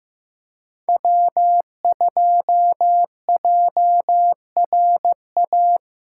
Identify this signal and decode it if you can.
Morse code, heard as a keyed tone, about 18 wpm, 700 Hz. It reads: W2JRA
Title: Speed Racing - USA 1x3 Call Signs 15wpm